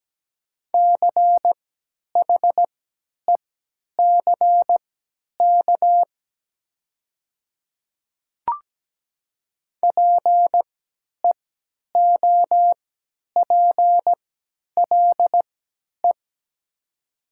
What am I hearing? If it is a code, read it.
Morse code, heard as a keyed tone, about 17 wpm, spaced as for 10 wpm, 700 Hz. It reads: CHECK PEOPLE
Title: 3x Character Spacing - Top 500 Words - 17wpm